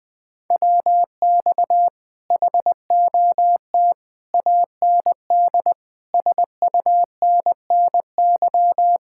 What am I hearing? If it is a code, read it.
Morse code, heard as a keyed tone, about 21 wpm, 700 Hz. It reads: WX HOT AND SUNNY
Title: Speed-Racing Common QSO Phrases Encore - 20wpm